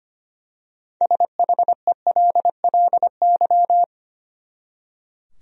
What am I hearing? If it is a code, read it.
Morse code, heard as a keyed tone, about 25 wpm, 700 Hz. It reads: SHELLY
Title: USA Common Names - 25wpm